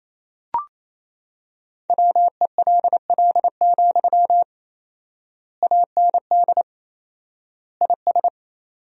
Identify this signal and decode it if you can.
Morse code, heard as a keyed tone, about 28 wpm, 700 Hz. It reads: WELL, AND IS
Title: 4x Word-Spacing - Sentences from Top 200 Words - 28wpm